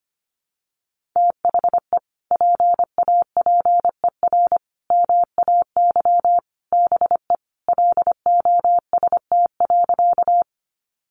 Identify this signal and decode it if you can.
Morse code, heard as a keyed tone, about 25 wpm, 700 Hz. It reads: THE PAPER MAY BE LOST.
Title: Sentences from Top 1400 Words 25wpm